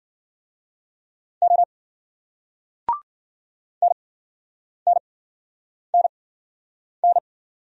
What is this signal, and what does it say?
Morse code, heard as a keyed tone, about 46 wpm, spaced as for 9 wpm, 700 Hz. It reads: O NNNN
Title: Speed-Racing Single Letters 40wpm